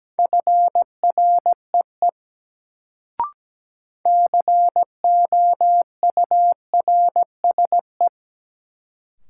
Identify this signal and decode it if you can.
Morse code, heard as a keyed tone, about 17 wpm, 700 Hz. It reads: FREE COURSE
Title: Sentences from Top 500 Words - New Word Drill - 17wpm